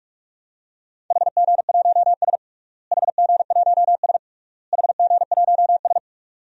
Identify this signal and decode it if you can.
Morse code, heard as a keyed tone, about 46 wpm, 700 Hz. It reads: HG1S HG1S HG1S
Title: DXCC-100 Call Signs - 4 Characters - Repeated 3x - 45wpm